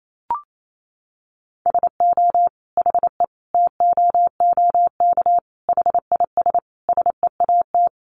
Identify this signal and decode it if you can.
Morse code, heard as a keyed tone, about 28 wpm, 700 Hz. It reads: SO HE TOOK HIS SEAT
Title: Sentences from Top 1500 Words 28wpm